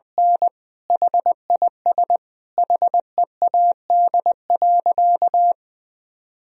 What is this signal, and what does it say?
Morse code, heard as a keyed tone, about 20 wpm, 700 Hz. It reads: N HIS HEAD.
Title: Sentences from Top 1000 Words - Review Part 3 - 20wpm